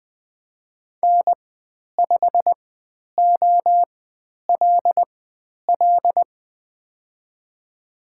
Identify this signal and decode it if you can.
Morse code, heard as a keyed tone, about 20 wpm, spaced as for 10 wpm, 700 Hz. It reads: N5OLL
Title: 095 Lesson 16g - Letter c - Call Signs 20wpm